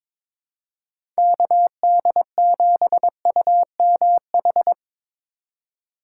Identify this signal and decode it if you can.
Morse code, heard as a keyed tone, about 22 wpm, 700 Hz. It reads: KD7UMH
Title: USA Call Signs - 6 Characters - 3x Character Spacing - 22wpm